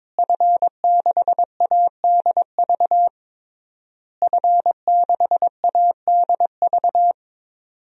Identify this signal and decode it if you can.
Morse code, heard as a keyed tone, about 22 wpm, 700 Hz. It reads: F6ADV F6ADV
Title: DXCC-100 Call Signs - 5 Characters - Repeated 3x - 22wpm